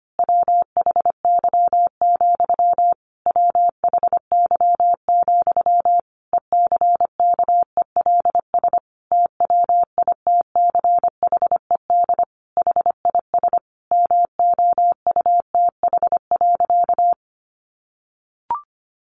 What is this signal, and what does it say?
Morse code, heard as a keyed tone, about 25 wpm, 700 Hz. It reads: WHY, WHY, ECKELS TWITCHED HIS MOUTH.
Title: A Sound Of Thunder 25wpm